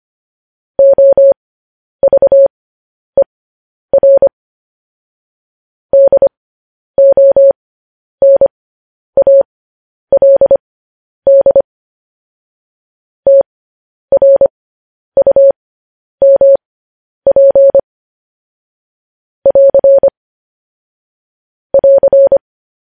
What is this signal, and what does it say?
Morse code, heard as a keyed tone, about 25 wpm, spaced as for 10 wpm, 550 Hz. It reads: OVER DONALD TRUMP <AR> <AR>
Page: news-headlines-in-morse-code-at-10-wpm